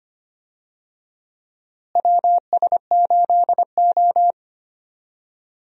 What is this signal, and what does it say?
Morse code, heard as a keyed tone, about 25 wpm, 700 Hz. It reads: WS8O
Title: USA Call Signs - 4 Characters - Repeated 2x - 25wpm